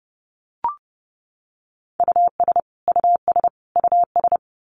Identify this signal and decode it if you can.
Morse code, heard as a keyed tone, about 30 wpm, 700 Hz. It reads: US US US
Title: Top 200 Words Repeated 30wpm